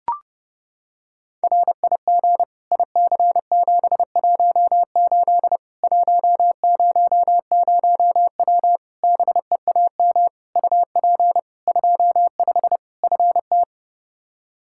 Morse code, heard as a keyed tone, about 30 wpm, 700 Hz. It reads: RIG IC718 100W BEAM UP 25 FT
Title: Practice QSO - Exchange 2 - Info - 30wpm